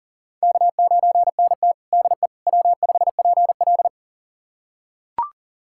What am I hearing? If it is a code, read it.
Morse code, heard as a keyed tone, about 37 wpm, 700 Hz. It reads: K9NT DE W5PL
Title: USA Callsign DE Callsign - 4 Characters - 40wpm